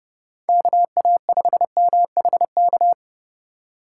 Morse code, heard as a keyed tone, about 32 wpm, 700 Hz. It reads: KA5MHK
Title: USA Call Signs - 6 Characters - 2x Character Spacing - 30wpm